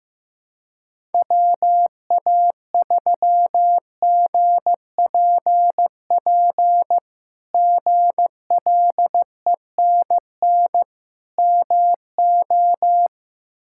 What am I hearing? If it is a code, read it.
Morse code, heard as a keyed tone, about 15 wpm, 700 Hz. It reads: WA3GPP GLENN MO
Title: K1USN SST QSO Practice - 15wpm